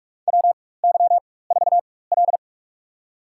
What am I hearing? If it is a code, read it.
Morse code, heard as a keyed tone, about 46 wpm, spaced as for 21 wpm, 700 Hz. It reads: WY4L
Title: USA Call Signs - 4 Characters - 4x Character Spacing - 45wpm